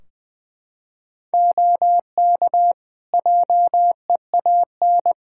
Morse code, heard as a keyed tone, about 20 wpm, 700 Hz. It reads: OK JEAN
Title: Practice QSO - Exchange 4 - Ending - 20wpm